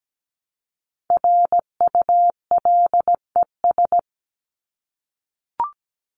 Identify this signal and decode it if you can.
Morse code, heard as a keyed tone, about 17 wpm, 700 Hz. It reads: RULES
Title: ES:S Suffix - Top 100 Words with Base Words 17wpm